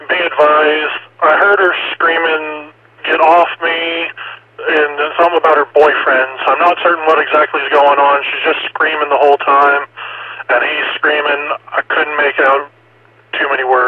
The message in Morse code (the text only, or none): none